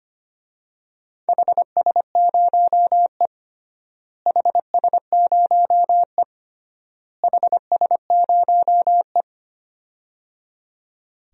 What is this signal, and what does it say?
Morse code, heard as a keyed tone, about 25 wpm, 700 Hz. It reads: HS0E HS0E HS0E
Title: DXCC-100 Call Signs - 4 Characters - Repeated 3x - 25wpm